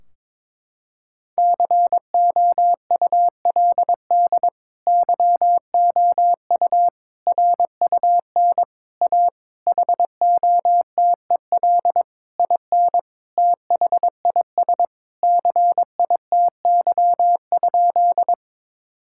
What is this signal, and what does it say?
Morse code, heard as a keyed tone, about 21 wpm, 700 Hz. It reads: COULD YOU RUN A HOTEL IN THIS CITY?
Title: Sentences from Top 1600 Words 22wpm